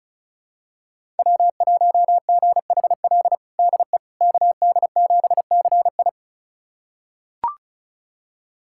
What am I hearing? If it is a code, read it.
Morse code, heard as a keyed tone, about 36 wpm, 700 Hz. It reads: W1GHL DE KD7CI
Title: USA Callsign DE Callsign - 5 Characters - 35wpm